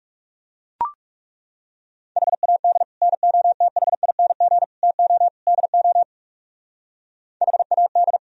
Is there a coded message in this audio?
Yes: Morse code from 2.2 to 8.2 s, heard as a keyed tone, about 38 wpm, 700 Hz. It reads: HAD NOTHING TO DO HAD